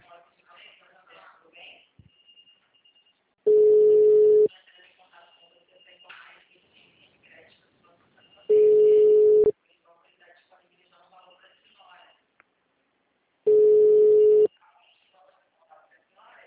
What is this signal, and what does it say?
Ringback tone (Europe, 425 Hz), with speech.